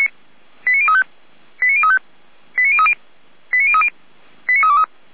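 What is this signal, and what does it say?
Five-tone selective call (ZVEI), heard as quick runs of five tones, tones of about 70 ms. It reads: 89035 89035 89030 89030 89323
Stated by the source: ZVEI five-tone call received by SDR, logged by the fire brigade Feuerwehr Orlamünde.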